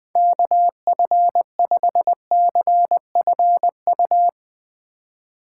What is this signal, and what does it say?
Morse code, heard as a keyed tone, about 20 wpm, 700 Hz. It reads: KF5CFU
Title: USA Call Signs - 6 Characters - 3x Character Spacing - 20wpm